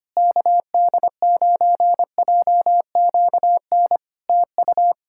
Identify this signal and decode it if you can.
Morse code, heard as a keyed tone, about 25 wpm, 700 Hz. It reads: KD9JQN TU
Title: Practice QSO - Exchange 4 - Ending - 25wpm